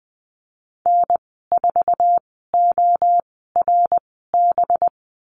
Morse code, heard as a keyed tone, about 20 wpm, spaced as for 14 wpm, 700 Hz. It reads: N4ORB
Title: 186 Lesson 30g - Letter b - Call Signs 20wpm